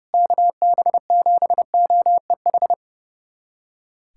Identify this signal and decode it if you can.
Morse code, heard as a keyed tone, about 32 wpm, 700 Hz. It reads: KB7OEH
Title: USA Callsign DE Callsign - 6 Characters - 30wpm